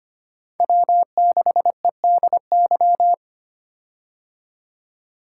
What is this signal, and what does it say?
Morse code, heard as a keyed tone, about 25 wpm, 700 Hz. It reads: W6EDY
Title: K1USN SST QSO Practice - 25wpm